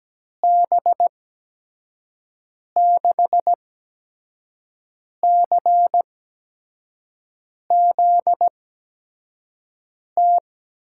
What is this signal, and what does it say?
Morse code, heard as a keyed tone, about 17 wpm, spaced as for 5 wpm, 700 Hz. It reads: B6CZT
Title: USA Call Signs - 6 Characters - 8x Character Spacing - 17wpm